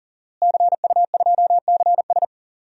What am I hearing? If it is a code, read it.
Morse code, heard as a keyed tone, about 40 wpm, 700 Hz. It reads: CU2CS
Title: DXCC-100 Call Signs - 5 Characters - Repeated 3x - 40wpm